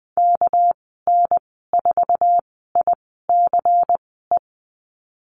Morse code, heard as a keyed tone, about 19 wpm, spaced as for 14 wpm, 700 Hz. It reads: KN4ICE
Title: 219 Lesson 35g - Number 0 - Call Signs 20wpm